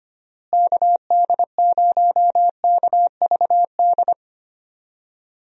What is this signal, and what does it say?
Morse code, heard as a keyed tone, about 25 wpm, 700 Hz. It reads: KD0KVD
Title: USA Call Signs - 6 Characters - Repeated 2x - 25wpm